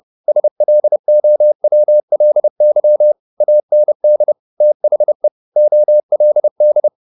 Morse code, heard as a keyed tone, about 30 wpm, 600 Hz. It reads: SLOWLY AND THE OLD